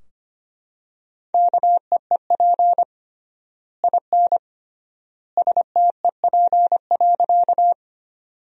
Morse code, heard as a keyed tone, about 25 wpm, 700 Hz. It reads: KEEP IN STEP.
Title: 3x Word-Spacing - Sentences from Top 400 Words - 25wpm